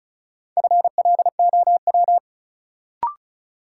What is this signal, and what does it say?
Morse code, heard as a keyed tone, about 36 wpm, 700 Hz. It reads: FLOW
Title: Sentences from Top 1200 Words - New Word Drill - 35wpm